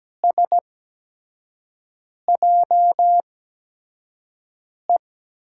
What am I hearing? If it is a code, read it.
Morse code, heard as a keyed tone, about 17 wpm, spaced as for 5 wpm, 700 Hz. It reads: SJE
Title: USA Call Signs - 6 Characters - 8x Character Spacing - 17wpm